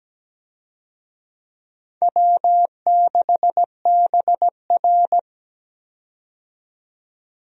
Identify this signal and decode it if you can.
Morse code, heard as a keyed tone, about 17 wpm, 700 Hz. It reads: W6BR
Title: USA Call Signs - 4 Characters - 17wpm